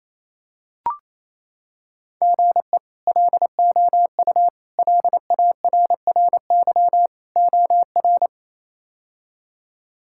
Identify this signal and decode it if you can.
Morse code, heard as a keyed tone, about 28 wpm, 700 Hz. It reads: GE LOU LARRY OR
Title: K1USN SST QSO Practice - 28wpm